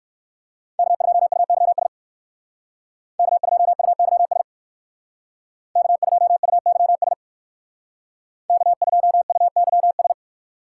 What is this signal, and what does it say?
Morse code, heard as a keyed tone, about 58 wpm, 700 Hz. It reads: K2UYS K2UYS K2UYS K2UYS